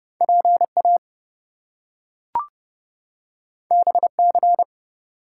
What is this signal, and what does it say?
Morse code, heard as a keyed tone, about 30 wpm, 700 Hz. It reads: PA BC
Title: US States and Canadian Provinces 30wpm